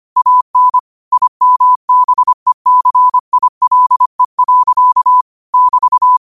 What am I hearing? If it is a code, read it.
Morse code, heard as a keyed tone, about 25 wpm, 1000 Hz. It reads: AN IMBECILE. <BT>